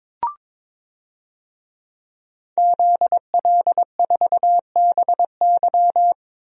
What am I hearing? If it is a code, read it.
Morse code, heard as a keyed tone, about 22 wpm, 700 Hz. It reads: ZL4BY
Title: DXCC-100 Call Signs - 5 Characters - Repeated 2x - 22wpm